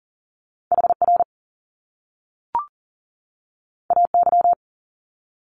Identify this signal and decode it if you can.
Morse code, heard as a keyed tone, about 38 wpm, 700 Hz. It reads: HR AY
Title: Top Two Letter Combinations in Words 40wpm